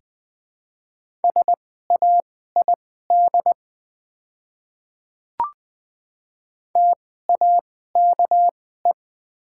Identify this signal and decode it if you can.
Morse code, heard as a keyed tone, about 20 wpm, spaced as for 14 wpm, 700 Hz. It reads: SAID TAKE
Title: 2x Character Spacing - Top 200 Words - 20wpm